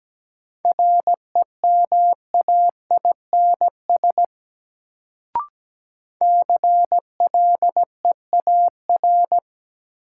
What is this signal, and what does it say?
Morse code, heard as a keyed tone, about 17 wpm, 700 Hz. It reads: REMAINS CLEAR
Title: Sets of 5 Words Encore - New Word Drill - 17wpm